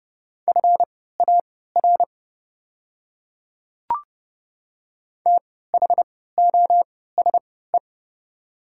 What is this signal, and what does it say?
Morse code, heard as a keyed tone, about 31 wpm, spaced as for 17 wpm, 700 Hz. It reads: FAR THOSE
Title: 3x Character Spacing - Top 300 Words - 30wpm